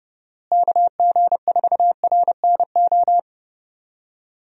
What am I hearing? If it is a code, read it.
Morse code, heard as a keyed tone, about 30 wpm, 700 Hz. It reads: KG4RNO
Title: USA Call Signs - 6 Characters - Repeated 3x - 30wpm